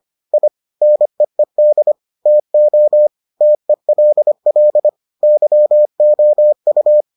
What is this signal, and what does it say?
Morse code, heard as a keyed tone, about 25 wpm, 600 Hz. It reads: I NEED TO TELL YOU